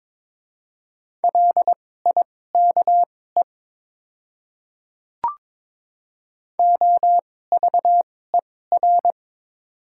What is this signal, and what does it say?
Morse code, heard as a keyed tone, about 23 wpm, spaced as for 16 wpm, 700 Hz. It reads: LIKE OVER
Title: 2x Character Spacing - Top 100 Words - 22wpm